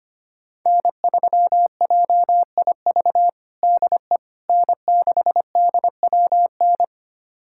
Morse code, heard as a keyed tone, about 25 wpm, 700 Hz. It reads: N3JIV DE N6DWN
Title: USA Callsign DE Callsign - 5 Characters - 25wpm